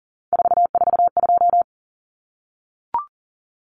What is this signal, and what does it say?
Morse code, heard as a keyed tone, about 40 wpm, 700 Hz. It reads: 442
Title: Numbers - 3-Digit 40wpm